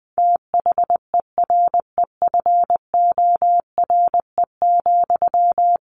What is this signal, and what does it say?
Morse code, heard as a keyed tone, about 20 wpm, 700 Hz. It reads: THEREFORE,
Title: Sentences from Top 800 Words 20wpm